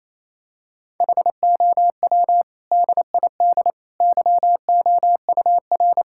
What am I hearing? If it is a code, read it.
Morse code, heard as a keyed tone, about 28 wpm, 700 Hz. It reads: HOW DID YOUR
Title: Sentences from Top 1000 Words - Review Part 6 - 28wpm